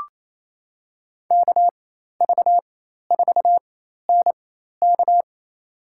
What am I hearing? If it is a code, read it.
Morse code, heard as a keyed tone, about 27 wpm, spaced as for 13 wpm, 700 Hz. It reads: KV4NK
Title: USA Call Signs - 5 Characters - 4x Character Spacing - 28wpm